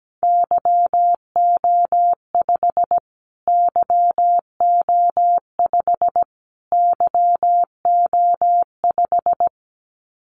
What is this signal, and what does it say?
Morse code, heard as a keyed tone, about 17 wpm, 700 Hz. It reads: YO5 YO5 YO5
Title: Callsign Prefixes for Top 25 DXCC - Repeated 3x - 17wpm